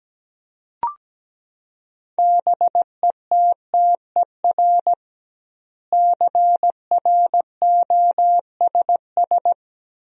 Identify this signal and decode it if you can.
Morse code, heard as a keyed tone, about 17 wpm, 700 Hz. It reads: BETTER CROSS
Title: 2x Word-Spacing - Sentences from Top 400 Words - 17wpm